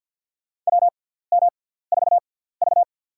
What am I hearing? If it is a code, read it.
Morse code, heard as a keyed tone, about 48 wpm, spaced as for 17 wpm, 690 Hz. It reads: WM4V
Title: USA Call Signs - 4 Characters - 6x Character Spacing - 50wpm